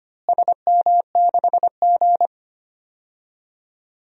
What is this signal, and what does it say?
Morse code, heard as a keyed tone, about 25 wpm, 700 Hz. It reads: SM6G
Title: DXCC-100 Call Signs - 4 Characters - Repeated 3x - 25wpm